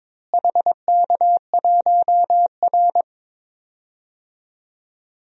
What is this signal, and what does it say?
Morse code, heard as a keyed tone, about 22 wpm, 700 Hz. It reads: HK1R
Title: DXCC-100 Call Signs - 4 Characters - Repeated 3x - 22wpm